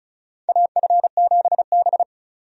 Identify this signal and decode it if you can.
Morse code, heard as a keyed tone, about 34 wpm, 700 Hz. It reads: AF7B
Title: USA Call Signs - 4 Characters - 35wpm